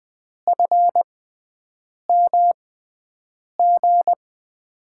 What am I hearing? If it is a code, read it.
Morse code, heard as a keyed tone, about 20 wpm, spaced as for 7 wpm, 700 Hz. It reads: FMG
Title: USA Call Signs - 6x Character Spacing - 20wpm